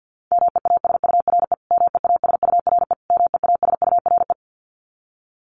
Morse code, heard as a keyed tone, about 50 wpm, 700 Hz. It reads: MEASURE MEASURE MEASURE
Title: Top 500 Words Repeated 50wpm